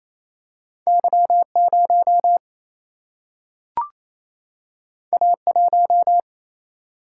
Morse code, heard as a keyed tone, about 28 wpm, 700 Hz. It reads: Y 0 A 1
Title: Single-Letter Single-Number 28wpm